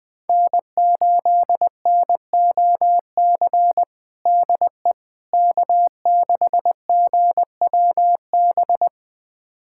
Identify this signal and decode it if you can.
Morse code, heard as a keyed tone, about 20 wpm, 700 Hz. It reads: N8NOC DE K6GWB
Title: USA Callsign DE Callsign - 5 Characters - 20wpm